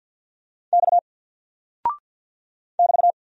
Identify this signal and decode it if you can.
Morse code, heard as a keyed tone, about 48 wpm, 700 Hz. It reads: X <BT>
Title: Single Character (Letter:Number:Punctuation:Prosign) Alt 50wpm